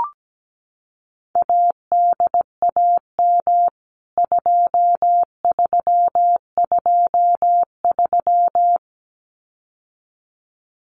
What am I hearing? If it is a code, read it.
Morse code, heard as a keyed tone, about 17 wpm, 700 Hz. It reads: ADAM 2323
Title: CWT Exchange 17wpm